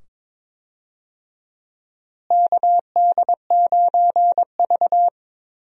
Morse code, heard as a keyed tone, about 21 wpm, 700 Hz. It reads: KD9V
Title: USA Call Signs - 4 Characters - Repeated 2x - 22wpm